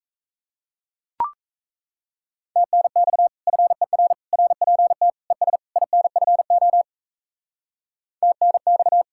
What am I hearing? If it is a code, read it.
Morse code, heard as a keyed tone, about 34 wpm, 700 Hz. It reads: TNX FER RPT ES INFO TNX
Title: Speed-Racing Common QSO Phrases - 28wpm